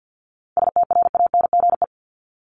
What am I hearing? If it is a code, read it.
Morse code, heard as a keyed tone, about 50 wpm, 700 Hz. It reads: STRANGE
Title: Top 1501-1750 Words 50wpm